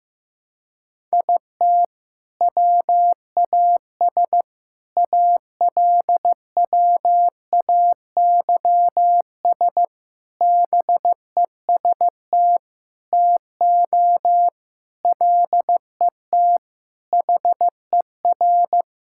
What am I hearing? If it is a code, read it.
Morse code, heard as a keyed tone, about 16 wpm, 700 Hz. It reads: IT WAS ALWAYS BEST TO LET HER
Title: Sentences from Top 1000 Words - Review Part 6 - 15wpm